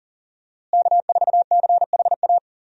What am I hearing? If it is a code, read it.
Morse code, heard as a keyed tone, about 40 wpm, 700 Hz. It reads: K4CHA
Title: USA Call Signs - 5 Characters - Repeated 2x - 40wpm